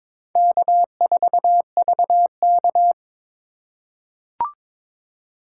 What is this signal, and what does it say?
Morse code, heard as a keyed tone, about 22 wpm, 690 Hz. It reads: K4VK
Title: USA Call Signs - 4 Characters - 3x Character Spacing - 22wpm